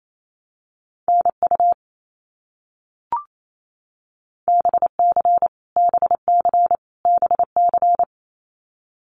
Morse code, heard as a keyed tone, about 28 wpm, 700 Hz. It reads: NU BC BC BC
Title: Canadian Provinces and Territories - Repeated 3x - 28wpm